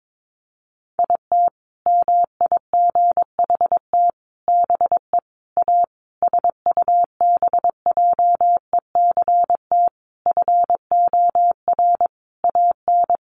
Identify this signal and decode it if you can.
Morse code, heard as a keyed tone, about 22 wpm, 700 Hz. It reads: IT MIGHT BE A SUBJECT FOR AN
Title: Sentences from Top 1500 Words 22wpm